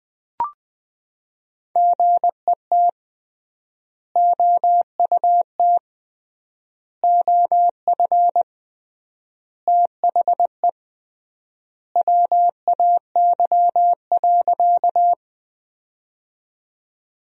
Morse code, heard as a keyed tone, about 21 wpm, 700 Hz. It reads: GET OUT OF THE WAY.
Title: 3x Word-Spacing - Sentences from Top 200 Words - 20wpm